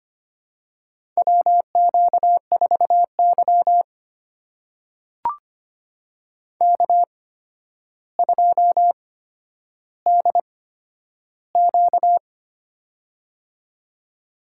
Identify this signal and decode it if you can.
Morse code, heard as a keyed tone, about 25 wpm, spaced as for 11 wpm, 700 Hz. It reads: WQ4Y K2DQ
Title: USA Call Signs - 4 Characters - 8x Character Spacing - 25wpm